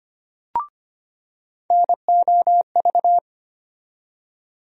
Morse code, heard as a keyed tone, about 25 wpm, 700 Hz. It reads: NOV
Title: USA Call Sign Postfixes - 3 Letters 25wpm